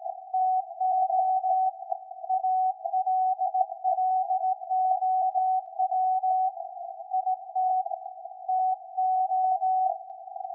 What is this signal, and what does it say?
Morse code, heard as a keyed tone, about 17 wpm, 750 Hz. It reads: TO ALLOW IT TO
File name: iz7slz / VLF / SAQ 13-02-2023